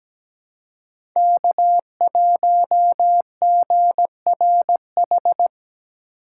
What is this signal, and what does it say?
Morse code, heard as a keyed tone, about 17 wpm, 700 Hz. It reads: K1GRH
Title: USA Call Signs - 5 Characters - 6x Character Spacing - 17wpm